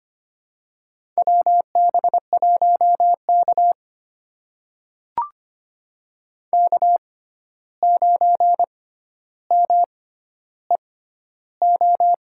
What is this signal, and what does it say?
Morse code, heard as a keyed tone, about 25 wpm, spaced as for 12 wpm, 700 Hz. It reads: WB1K K9MEO
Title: USA Call Signs - 6x Character Spacing - 25wpm